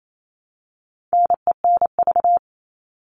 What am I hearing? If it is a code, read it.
Morse code, heard as a keyed tone, about 28 wpm, 700 Hz. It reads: NENV
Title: All Cut Numbers - 4-Digit 28wpm